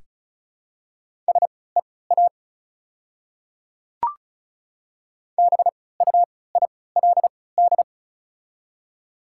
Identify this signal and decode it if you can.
Morse code, heard as a keyed tone, about 34 wpm, spaced as for 20 wpm, 690 Hz. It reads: SEA BUILD
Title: 3x Character Spacing - Top 300 Words - 35wpm